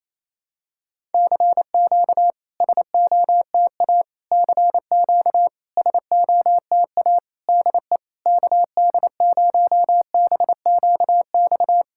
Morse code, heard as a keyed tone, about 28 wpm, 690 Hz. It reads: CQ SOTA CQ SOTA DE KD0BQX
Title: SOTA QSO Practice 28wpm